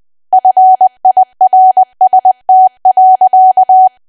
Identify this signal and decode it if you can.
Morse code, heard as a keyed tone, about 20 wpm, 750 Hz. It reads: FIRST.
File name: cwa / W1AW-Level3 / 20WPM